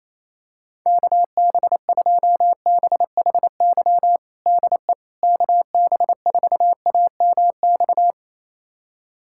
Morse code, heard as a keyed tone, about 28 wpm, 700 Hz. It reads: KB2BHY DE KB4AMX
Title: USA Callsign DE Callsign - 6 Characters - 28wpm